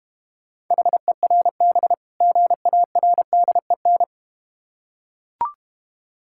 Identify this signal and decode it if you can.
Morse code, heard as a keyed tone, about 30 wpm, 700 Hz. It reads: HERB GARDEN
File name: practice / 25wpm / Difficult Words - Speed-Racing with Hints - 25wpm